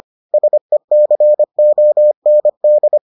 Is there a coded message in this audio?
Yes: Morse code from 0.3 to 3.0 s, heard as a keyed tone, about 25 wpm, 600 Hz. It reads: SECOND